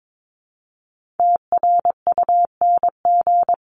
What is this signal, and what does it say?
Morse code, heard as a keyed tone, about 22 wpm, 700 Hz. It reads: TRUNG
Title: CWT Exchange 22wpm